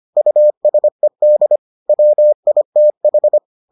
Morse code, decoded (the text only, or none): USED WITH